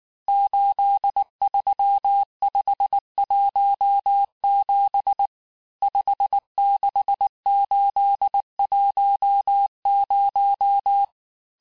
Morse code, heard as a keyed tone, about 19 wpm, 800 Hz. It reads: 83517 56810
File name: cours-du-mercredi / audios / cours-869-texte-CHIFFRE-vitesse-18-MPM